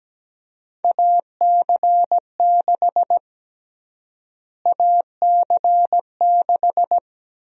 Morse code, heard as a keyed tone, about 17 wpm, 700 Hz. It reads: AC6 AC6
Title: USA Call Sign Prefixes - Repeated 2x - 17wpm